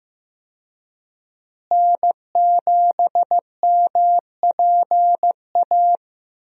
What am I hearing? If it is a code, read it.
Morse code, heard as a keyed tone, about 15 wpm, 700 Hz. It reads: N7MPA
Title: USA Call Signs - 5 Characters - Repeated 3x - 15wpm